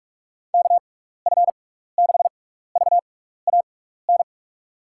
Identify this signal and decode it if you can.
Morse code, heard as a keyed tone, about 46 wpm, spaced as for 16 wpm, 700 Hz. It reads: KF6VAN
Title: USA Call Signs - 6 Characters - 6x Character Spacing - 45wpm